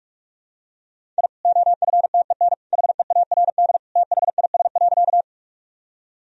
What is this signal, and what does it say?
Morse code, heard as a keyed tone, about 47 wpm, 700 Hz. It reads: I OFTEN HEARD THIS.
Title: Sentences from Top 1000 Words - Review Part 1 - 45wpm